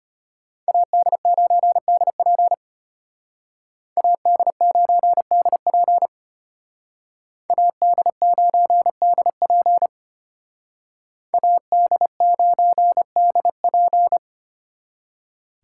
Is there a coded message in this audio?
Yes: Morse code from 0.7 to 14.2 s, heard as a keyed tone, about 32 wpm, 700 Hz. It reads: AD9DP AD9DP AD9DP AD9DP